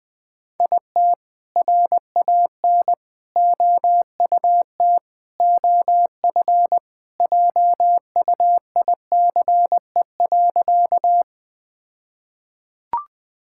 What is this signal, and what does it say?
Morse code, heard as a keyed tone, about 20 wpm, 700 Hz. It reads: IT RAN OUT OF JUICE.
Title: All-Age Appropriate Jokes - 20wpm